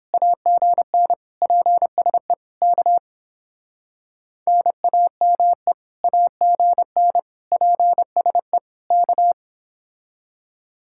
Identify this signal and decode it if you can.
Morse code, heard as a keyed tone, about 27 wpm, 700 Hz. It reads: AGN PSE K NAME AGN PSE K
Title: Speed-Racing Common QSO Phrases - 22wpm